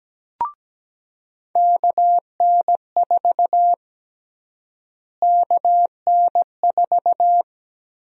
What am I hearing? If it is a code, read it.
Morse code, heard as a keyed tone, about 17 wpm, 700 Hz. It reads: KN4 KN4
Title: USA Call Sign Prefixes - Repeated 2x - 17wpm